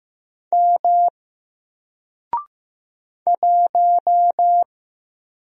Morse code, heard as a keyed tone, about 15 wpm, 700 Hz. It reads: M 1
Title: Single Letter-Number 15wpm